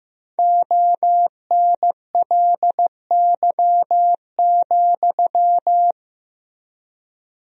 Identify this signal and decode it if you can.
Morse code, heard as a keyed tone, about 15 wpm, 700 Hz. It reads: ONLY,
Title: 3x Word-Spacing - Sentences from Top 300 Words - 15wpm